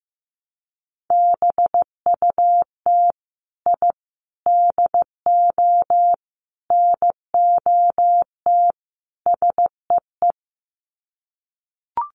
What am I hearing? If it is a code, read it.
Morse code, heard as a keyed tone, about 15 wpm, 700 Hz. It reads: BUT I DO NOT SEE
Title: Sets of 5 Words Encore 15wpm